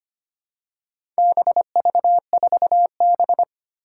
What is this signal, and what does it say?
Morse code, heard as a keyed tone, about 25 wpm, 700 Hz. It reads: BV4B
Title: Four Challenging Letter-Number Combinations 25wpm